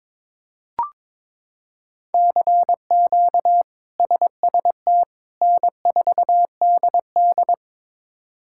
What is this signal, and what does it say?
Morse code, heard as a keyed tone, about 21 wpm, 700 Hz. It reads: CQ SST N4DD
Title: K1USN SST QSO Practice - 22wpm